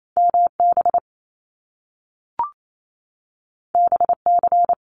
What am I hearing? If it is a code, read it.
Morse code, heard as a keyed tone, about 28 wpm, 700 Hz. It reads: MB BC
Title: Canadian Provinces and Territories 28wpm